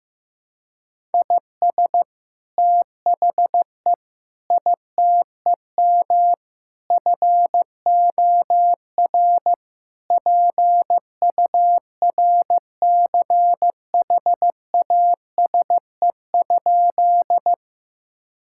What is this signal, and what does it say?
Morse code, heard as a keyed tone, about 15 wpm, 700 Hz. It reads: IS THE ITEM FOR PURCHASE?
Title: Sentences from Top 1000 Words - Review Part 6 - 15wpm